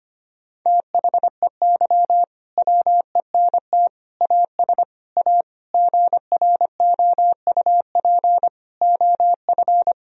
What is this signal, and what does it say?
Morse code, heard as a keyed tone, about 25 wpm, 700 Hz. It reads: THEY WENT AS A GROUP OF
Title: Sentences from Top 1000 Words - Review Part 1 - 25wpm